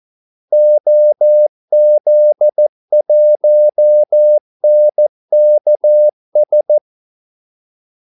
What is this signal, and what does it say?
Morse code, heard as a keyed tone, about 14 wpm, 600 Hz. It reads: OZ1NKS
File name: Callsigns-14wpm